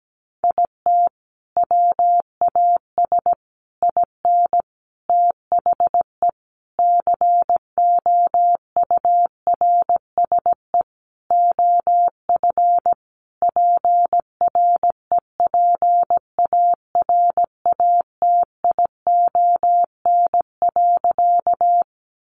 Morse code, heard as a keyed tone, about 17 wpm, 700 Hz. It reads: IT WAS IN THE COURSE OF PREPARATION.
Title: Sentences from Top 1400 Words 17wpm